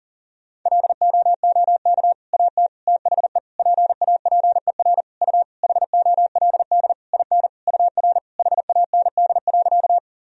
Morse code, heard as a keyed tone, about 37 wpm, 700 Hz. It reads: LOOK AT THE PAPER U HOLD IN UR HAND.
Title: Sentences from Top 400 Words and Abbreviations 40wpm